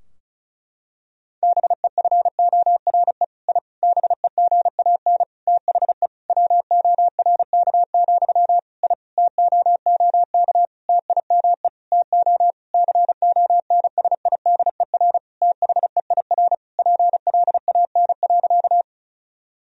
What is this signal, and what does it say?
Morse code, heard as a keyed tone, about 34 wpm, 700 Hz. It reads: BEFORE I BEGAN THE WORK, I TOOK TIME TO CONSIDER THEIR PLAN.
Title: Sentences from Top 1000 Words - Review Part 4 - 35wpm